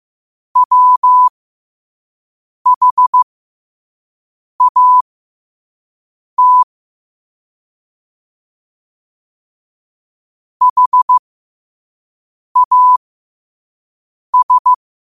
Morse code, heard as a keyed tone, about 15 wpm, spaced as for 5 wpm, 1000 Hz. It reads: WHAT HAS
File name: qotdfiles / 5wpm